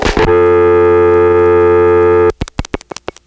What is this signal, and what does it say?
Dial tone (North America), 350 + 440 Hz, with noise.
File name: phonetrips / PrinceWmCoVA5HQ